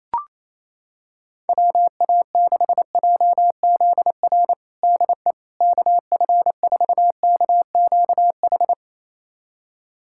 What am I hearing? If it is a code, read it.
Morse code, heard as a keyed tone, about 28 wpm, 700 Hz. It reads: WA6JZR DE KF4KQH
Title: USA Callsign DE Callsign - 6 Characters - 28wpm